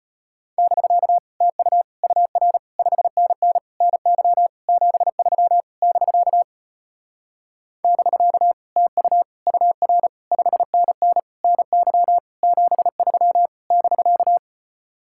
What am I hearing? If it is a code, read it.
Morse code, heard as a keyed tone, about 33 wpm, 700 Hz. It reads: <BK> TU UR 5NN NY 73 <BK> <BK> TU UR 5NN NY 73 <BK>